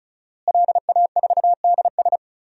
Morse code, heard as a keyed tone, about 36 wpm, 700 Hz. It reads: LA4DS
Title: DXCC-100 Call Signs - 5 Characters - Repeated 3x - 35wpm